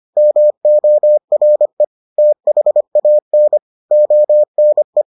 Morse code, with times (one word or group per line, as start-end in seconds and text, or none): 0.2-1.8 MORE
2.2-3.6 THAN
3.9-5.0 ONE